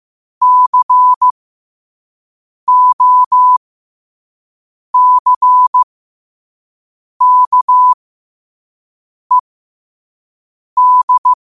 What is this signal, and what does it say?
Morse code, heard as a keyed tone, about 15 wpm, spaced as for 5 wpm, 1000 Hz. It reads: COCKED
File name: qotdfiles / 5wpm